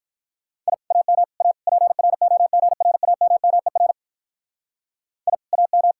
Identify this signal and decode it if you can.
Morse code, heard as a keyed tone, about 44 wpm, 700 Hz. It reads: I AM A PROGRAMMER I AM